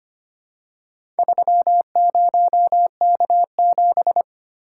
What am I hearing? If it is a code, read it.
Morse code, heard as a keyed tone, about 25 wpm, 700 Hz. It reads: 30K7
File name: practice / 25wpm / Four Letter-Number Combinations - 25wpm